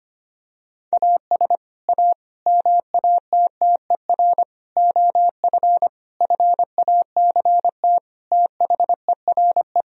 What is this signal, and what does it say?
Morse code, heard as a keyed tone, about 25 wpm, 700 Hz. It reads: AS A MATTER OF FACT THERE
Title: Sentences from Top 1000 Words - Review Part 3 - 25wpm